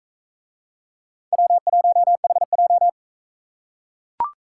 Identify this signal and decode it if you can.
Morse code, heard as a keyed tone, about 40 wpm, 700 Hz. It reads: W1HJ